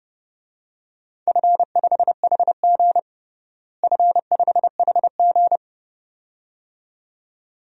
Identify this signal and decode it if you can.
Morse code, heard as a keyed tone, about 29 wpm, 700 Hz. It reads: F5HG F5HG
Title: DXCC-100 Call Signs - 4 Characters - Repeated 2x - 30wpm